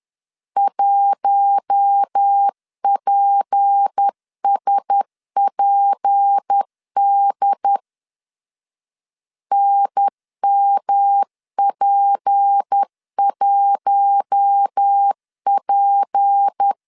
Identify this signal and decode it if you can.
Morse code, heard as a keyed tone, about 10 wpm, 800 Hz. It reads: 1PSPD NMP1P
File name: LEZIONI / 6 PKX0 / 6 P 10wpm 2432